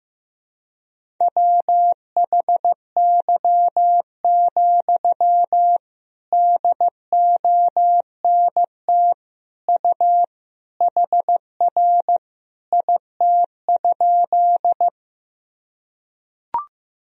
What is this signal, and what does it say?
Morse code, heard as a keyed tone, about 16 wpm, 700 Hz. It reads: WHY, DONT U HR IT?
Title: Sentences from Top 300 Words and Abbreviations 15wpm